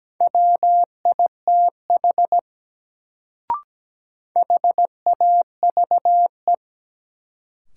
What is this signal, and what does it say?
Morse code, heard as a keyed tone, about 17 wpm, 700 Hz. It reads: WITH HAVE
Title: Sets of 2 Words - New Word Drill - 17wpm